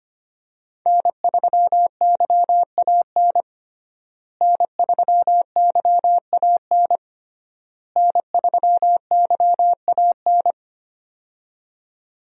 Morse code, heard as a keyed tone, about 25 wpm, 700 Hz. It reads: N3YAN N3YAN N3YAN
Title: USA Call Signs - 5 Characters - Repeated 3x - 25wpm